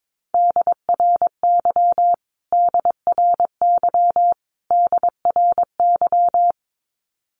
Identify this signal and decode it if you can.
Morse code, heard as a keyed tone, about 22 wpm, 700 Hz. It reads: DRY DRY DRY
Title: Top 500 Words Repeated 22wpm